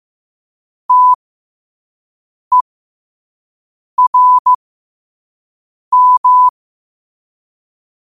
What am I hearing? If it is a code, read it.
Morse code, heard as a keyed tone, about 15 wpm, spaced as for 5 wpm, 1000 Hz. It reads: TERM
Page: New quotes every day in morse code at 5 Words per minute.